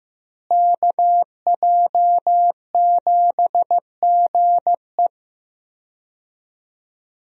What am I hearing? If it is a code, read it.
Morse code, heard as a keyed tone, about 16 wpm, 700 Hz. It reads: KJ7GE
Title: USA Call Signs - 5 Characters - Repeated 3x - 15wpm